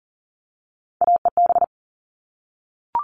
Morse code, heard as a keyed tone, about 40 wpm, 700 Hz. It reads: AEB
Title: All Cut Numbers - 3-Digit 40wpm